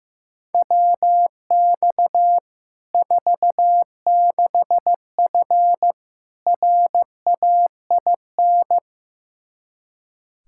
Morse code, heard as a keyed tone, about 16 wpm, 700 Hz. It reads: WX 46F RAIN
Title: Practice QSO - Exchange 2 - Info - 15wpm